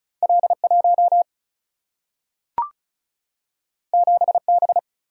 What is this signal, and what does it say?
Morse code, heard as a keyed tone, about 36 wpm, 700 Hz. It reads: L 1 7 B
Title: Single-Letter Single-Number 35wpm